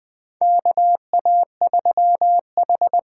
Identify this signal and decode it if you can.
Morse code, heard as a keyed tone, about 20 wpm, 700 Hz. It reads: KA3H
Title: USA Call Signs - 6 Characters - 8x Character Spacing - 20wpm